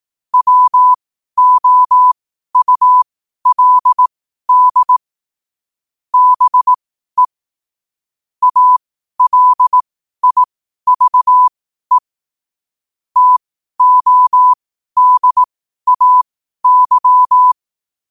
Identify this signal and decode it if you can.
Morse code, heard as a keyed tone, about 18 wpm, spaced as for 12 wpm, 1000 Hz. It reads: WOULD BE ALIVE TODAY
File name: qotdfiles / 12wpm